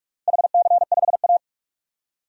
Morse code, heard as a keyed tone, about 46 wpm, 700 Hz. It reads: HC5A
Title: DXCC-100 Call Signs - 4 Characters - 45wpm